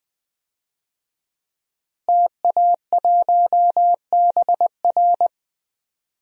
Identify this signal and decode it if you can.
Morse code, heard as a keyed tone, about 20 wpm, 700 Hz. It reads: TA1BR
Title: DXCC-100 Call Signs - 5 Characters - Repeated 2x - 20wpm